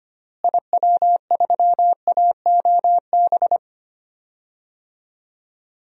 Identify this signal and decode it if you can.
Morse code, heard as a keyed tone, about 25 wpm, 700 Hz. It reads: IW3AOB
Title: DXCC-100 Call Signs - 6 Characters - Repeated 2x - 25wpm